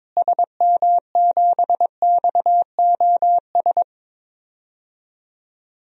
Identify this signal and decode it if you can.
Morse code, heard as a keyed tone, about 22 wpm, 700 Hz. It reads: SM7XOS
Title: DXCC-100 Call Signs - 6 Characters - Repeated 2x - 22wpm